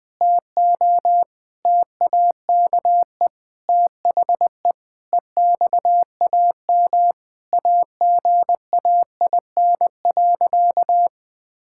Morse code, heard as a keyed tone, about 21 wpm, 700 Hz. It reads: TO TAKE THE EXAM AGAIN.
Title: Sentences from Top 1000 Words - Review Part 5 - 20wpm